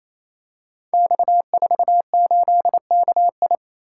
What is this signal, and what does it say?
Morse code, heard as a keyed tone, about 29 wpm, 700 Hz. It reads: X48KI
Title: Code Groups - 5 Letter:Numbers - 28wpm